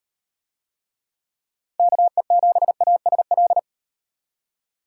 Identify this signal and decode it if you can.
Morse code, heard as a keyed tone, about 38 wpm, 700 Hz. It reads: KE7ASL